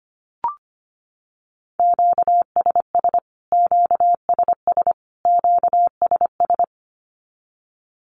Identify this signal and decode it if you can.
Morse code, heard as a keyed tone, about 25 wpm, 700 Hz. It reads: QSS QSS QSS
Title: Q-Codes Repeated 3x 25wpm